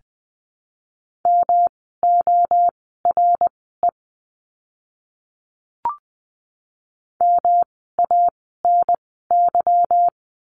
Morse code, heard as a keyed tone, about 20 wpm, spaced as for 14 wpm, 700 Hz. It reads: MORE MANY
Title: 237 Lesson 38f - Letter q - Words 20wpm